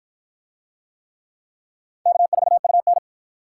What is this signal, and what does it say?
Morse code, heard as a keyed tone, about 49 wpm, 700 Hz. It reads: K4UN